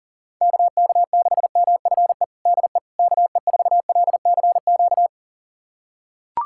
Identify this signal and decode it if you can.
Morse code, heard as a keyed tone, about 37 wpm, 700 Hz. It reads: KK6MFE DE KE4LCQ
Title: USA Callsign DE Callsign - 6 Characters - 40wpm